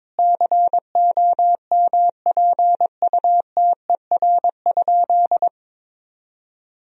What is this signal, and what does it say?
Morse code, heard as a keyed tone, about 22 wpm, 700 Hz. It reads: COMPUTER?
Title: Sentences from Top 1000 Words - Review Part 3 - 22wpm